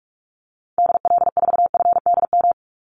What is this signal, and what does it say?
Morse code, heard as a keyed tone, about 44 wpm, 700 Hz. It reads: DL4FDM
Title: Call Signs Repeated 2x - 45wpm